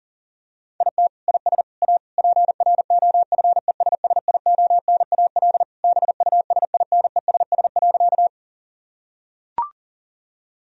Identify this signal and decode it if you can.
Morse code, heard as a keyed tone, about 40 wpm, 700 Hz. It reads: IT IS A PROFESSIONAL BUSINESS.
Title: Sentences from Top 1000 Words - Review Part 4 - 40wpm